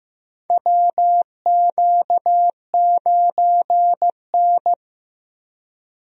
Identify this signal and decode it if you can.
Morse code, heard as a keyed tone, about 16 wpm, 700 Hz. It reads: WQ9N
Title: USA Call Signs - 4 Characters - Repeated 2x - 15wpm